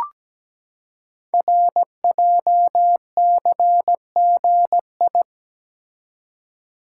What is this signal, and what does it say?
Morse code, heard as a keyed tone, about 17 wpm, 700 Hz. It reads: RJCGI
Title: Code Groups - 5 Letters - 17wpm